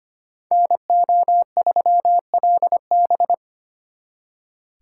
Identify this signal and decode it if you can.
Morse code, heard as a keyed tone, about 33 wpm, 700 Hz. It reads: NO3LB